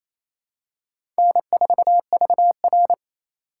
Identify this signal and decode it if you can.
Morse code, heard as a keyed tone, about 28 wpm, 690 Hz. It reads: N4VR
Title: USA Call Signs - 4 Characters - 2x Character Spacing - 28wpm